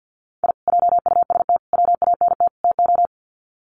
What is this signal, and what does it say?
Morse code, heard as a keyed tone, about 50 wpm, 700 Hz. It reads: I JUST WANT TO
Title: Sets of 4 Words 50wpm